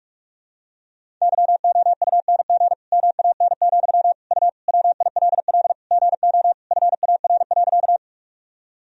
Morse code, heard as a keyed tone, about 41 wpm, 700 Hz. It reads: YOUNG MAN, U WILL GO FAR.
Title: Sentences from Top 400 Words and Abbreviations 45wpm